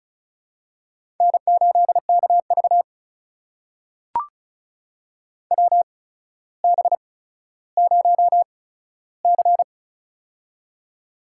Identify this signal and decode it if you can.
Morse code, heard as a keyed tone, about 36 wpm, spaced as for 15 wpm, 700 Hz. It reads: N8KV WB0C
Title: USA Call Signs - 4 Characters - 8x Character Spacing - 35wpm